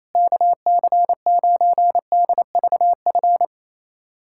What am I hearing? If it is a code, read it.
Morse code, heard as a keyed tone, about 28 wpm, 700 Hz. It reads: KC9DVF
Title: USA Callsign DE Callsign - 6 Characters - 28wpm